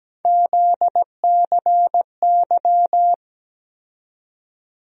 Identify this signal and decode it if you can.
Morse code, heard as a keyed tone, about 17 wpm, 700 Hz. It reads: ZCY
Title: USA Call Sign Postfixes - 3 Letters 17wpm